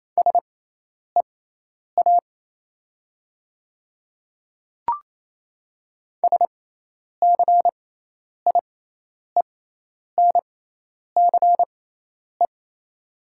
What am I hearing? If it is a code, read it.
Morse code, heard as a keyed tone, about 27 wpm, spaced as for 10 wpm, 690 Hz. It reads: SEA SCIENCE
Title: 6x Character Spacing - Top 500 Words - 28wpm